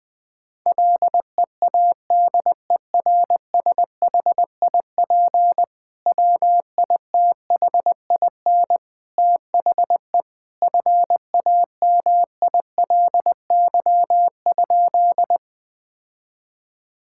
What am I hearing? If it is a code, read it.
Morse code, heard as a keyed tone, about 20 wpm, 700 Hz. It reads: LEADERSHIP WITHIN THE FAMILY?
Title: Sentences from Top 1600 Words 20wpm